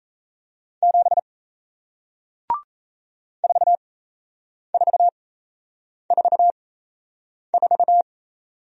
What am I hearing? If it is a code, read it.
Morse code, heard as a keyed tone, about 38 wpm, spaced as for 8 wpm, 700 Hz. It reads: 7 4444